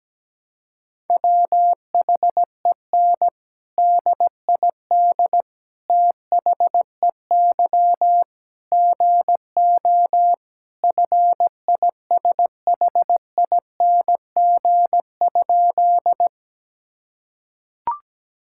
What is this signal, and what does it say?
Morse code, heard as a keyed tone, about 17 wpm, 700 Hz. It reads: WHEN DID THEY GO FISHING?
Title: Sentences from Top 1000 Words - Review Part 5 - 17wpm